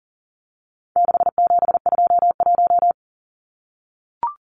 Morse code, heard as a keyed tone, about 40 wpm, 700 Hz. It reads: 6721
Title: Numbers - 4-Digit 40wpm